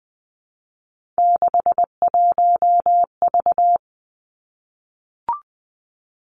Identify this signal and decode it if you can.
Morse code, heard as a keyed tone, about 20 wpm, 700 Hz. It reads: 61V
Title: Three Challenging Letter-Number Combinations 20wpm